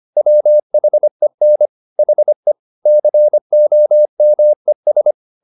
Morse code, heard as a keyed tone, about 25 wpm, 600 Hz. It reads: WHEN HE COMES